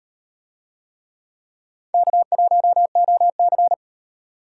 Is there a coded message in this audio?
Yes: Morse code from 1.9 to 3.7 s, heard as a keyed tone, about 38 wpm, 700 Hz. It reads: K1OC